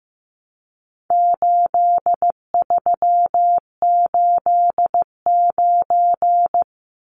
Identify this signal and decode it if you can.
Morse code, heard as a keyed tone, about 15 wpm, 700 Hz. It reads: 8389
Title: Numbers - 4-Digit 15wpm